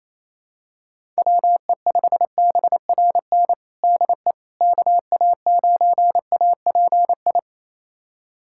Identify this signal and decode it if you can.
Morse code, heard as a keyed tone, about 28 wpm, 700 Hz. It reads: WE5BRN DE KA9API
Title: USA Callsign DE Callsign - 6 Characters - 28wpm